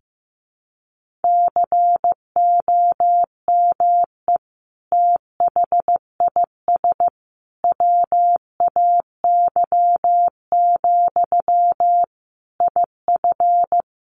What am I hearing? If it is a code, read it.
Morse code, heard as a keyed tone, about 15 wpm, 700 Hz. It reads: COME THIS WAY, IF
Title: Sentences from Top 1500 Words 15wpm